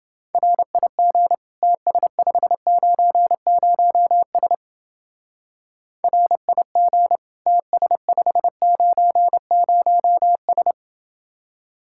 Morse code, heard as a keyed tone, about 28 wpm, 700 Hz. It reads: RIG TS590S RIG TS590S